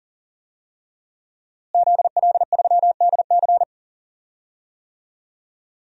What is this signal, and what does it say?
Morse code, heard as a keyed tone, about 40 wpm, 700 Hz. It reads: ZL3DC
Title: DXCC-100 Call Signs - 5 Characters - Repeated 3x - 40wpm